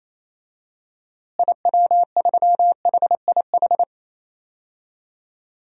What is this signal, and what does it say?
Morse code, heard as a keyed tone, about 28 wpm, 700 Hz. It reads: IW3HIH
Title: DXCC-100 Call Signs - 6 Characters - Repeated 2x - 28wpm